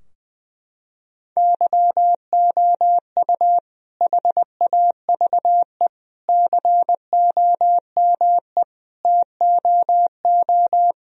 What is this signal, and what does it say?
Morse code, heard as a keyed tone, about 20 wpm, 700 Hz. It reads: YOU HAVE COME TOO
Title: Sentences from Top 1000 Words - Review Part 2 - 20wpm